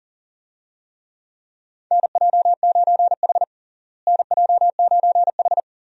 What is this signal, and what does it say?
Morse code, heard as a keyed tone, about 40 wpm, 700 Hz. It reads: NJ9H NJ9H
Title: USA Call Signs - 4 Characters - Repeated 3x - 40wpm